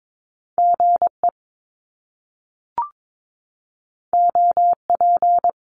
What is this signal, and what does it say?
Morse code, heard as a keyed tone, about 22 wpm, 700 Hz. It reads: GE OP
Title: Top Two Letter Combinations in Words 22wpm